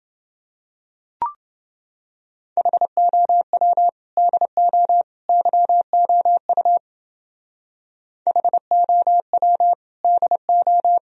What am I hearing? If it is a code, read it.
Morse code, heard as a keyed tone, about 28 wpm, 700 Hz. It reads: HOW DO YOU HOW DO